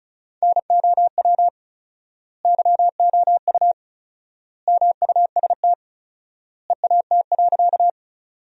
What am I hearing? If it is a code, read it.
Morse code, heard as a keyed tone, about 36 wpm, 700 Hz. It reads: NOW YOU MUST EAT.
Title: 4x Word-Spacing - Sentences from Top 300 Words - 35wpm